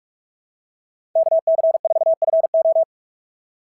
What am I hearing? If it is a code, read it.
Morse code, heard as a keyed tone, about 46 wpm, 690 Hz. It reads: KC4FO
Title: SOTA QSO Practice 45wpm